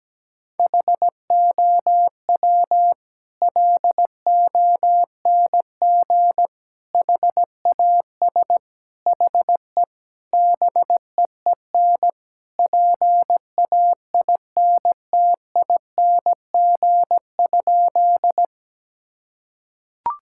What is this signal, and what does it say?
Morse code, heard as a keyed tone, about 17 wpm, 700 Hz. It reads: HOW LONG HAS HE BEEN PAINTING?
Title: Sentences from Top 1000 Words - Review Part 6 - 17wpm